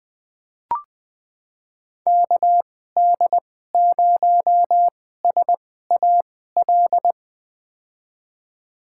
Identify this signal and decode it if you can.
Morse code, heard as a keyed tone, about 20 wpm, spaced as for 14 wpm, 700 Hz. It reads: KD0SAL
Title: USA Call Signs - 6 Characters - 2x Character Spacing - 20wpm